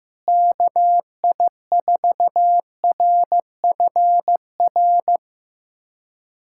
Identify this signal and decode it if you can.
Morse code, heard as a keyed tone, about 15 wpm, 700 Hz. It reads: KI4RFR
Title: USA Callsign DE Callsign 15wpm